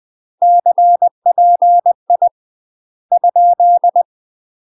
Morse code, heard as a keyed tone, about 20 wpm, 700 Hz. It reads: CPI ?
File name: UK MASTER QSOs TRAINING/20WPM